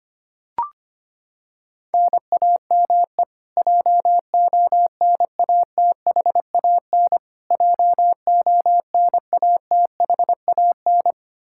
Morse code, heard as a keyed tone, about 25 wpm, 700 Hz. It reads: NAME JONATHAN JONATHAN
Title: Practice QSO - Exchange 1 - Beginning - 25wpm